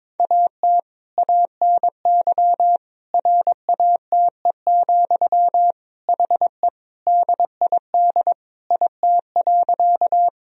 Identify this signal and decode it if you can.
Morse code, heard as a keyed tone, about 22 wpm, 700 Hz. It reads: AT ANY RATE, HE DID IT.
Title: Sentences from Top 1000 Words - Review Part 4 - 22wpm